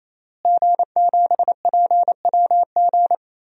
Morse code, heard as a keyed tone, about 28 wpm, 700 Hz. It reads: G7PWG
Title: DXCC-100 Call Signs - 5 Characters - Repeated 3x - 28wpm